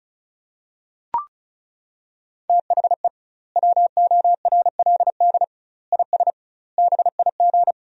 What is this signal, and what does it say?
Morse code, heard as a keyed tone, about 36 wpm, 700 Hz. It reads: THE WORLD IS BIG
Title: 2x Word-Spacing - Sentences from Top 400 Words - 35wpm